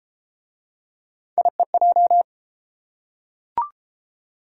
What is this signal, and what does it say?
Morse code, heard as a keyed tone, about 31 wpm, 700 Hz. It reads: IEJ